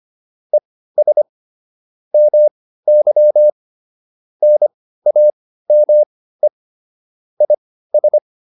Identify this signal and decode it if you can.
Morse code, heard as a keyed tone, about 25 wpm, spaced as for 15 wpm, 600 Hz. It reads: ES MY NAME IS